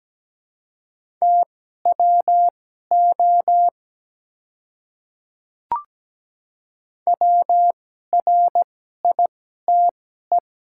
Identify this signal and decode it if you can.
Morse code, heard as a keyed tone, about 17 wpm, spaced as for 12 wpm, 700 Hz. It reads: TWO WRITE
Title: 2x Character Spacing - Top 200 Words - 17wpm